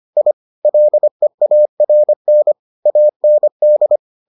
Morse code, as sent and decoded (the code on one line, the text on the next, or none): .. / .-.. .- .-. -. / .- -. -..
I LEARN AND